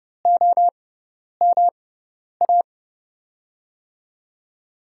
Morse code, heard as a keyed tone, about 30 wpm, spaced as for 10 wpm, 700 Hz. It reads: OMA
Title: USA Call Signs - 5 Characters - 6x Character Spacing - 30wpm